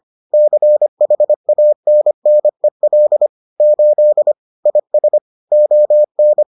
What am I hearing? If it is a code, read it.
Morse code, heard as a keyed tone, about 25 wpm, 600 Hz. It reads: CHANNEL 8 IS ON